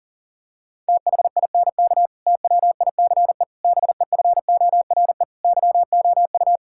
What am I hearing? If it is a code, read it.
Morse code, heard as a keyed tone, about 40 wpm, 700 Hz. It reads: THINK TWICE BEFORE YOU
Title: Sentences from Top 1600 Words 40wpm